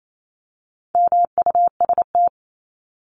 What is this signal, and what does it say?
Morse code, heard as a keyed tone, about 28 wpm, 700 Hz. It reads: MUST
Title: Top 500 Words 28wpm